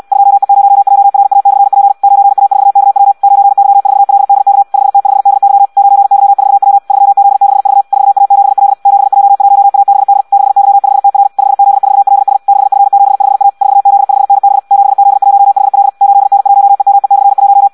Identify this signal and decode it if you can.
Morse code, heard as a keyed tone, about 74 wpm, 800 Hz. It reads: 2E0JRA/P 9A4KW 9Y4LDK 5T5DY 9Y4W 3Z6V 4A7L 7J1ABD 6Y5TR 5Z4LS 6F75A 4Z5NB 7Z1HL 8N1EME/1